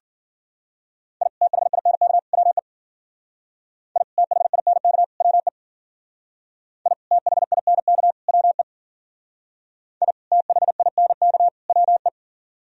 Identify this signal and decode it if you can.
Morse code, heard as a keyed tone, about 49 wpm, 700 Hz. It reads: I THINK WE I THINK WE I THINK WE I THINK WE
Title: Speed-Racing - Sets of 3 Words 40wpm